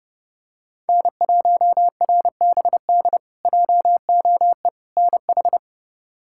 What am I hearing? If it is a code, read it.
Morse code, heard as a keyed tone, about 32 wpm, 700 Hz. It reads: N1RBD JOE NH
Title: K1USN SST QSO Practice - 30wpm